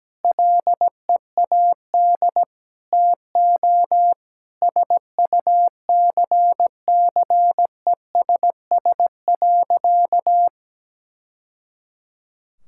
Morse code, heard as a keyed tone, about 17 wpm, 700 Hz. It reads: LEAD TO SUCCESS.
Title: Sentences from Top 1000 Words - Review Part 6 - 17wpm